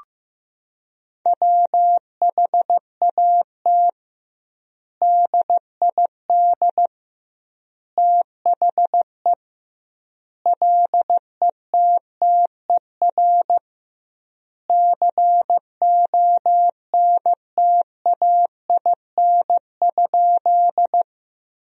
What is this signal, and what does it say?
Morse code, heard as a keyed tone, about 15 wpm, 700 Hz. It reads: WHAT DID THE LETTER CONTAIN?
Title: 2x Word-Spacing - Sentences from Top 500 Words - 15wpm